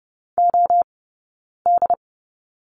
Morse code, heard as a keyed tone, about 30 wpm, 700 Hz. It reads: O D
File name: Lesson 19d - Letter w - Two Characters 30wpm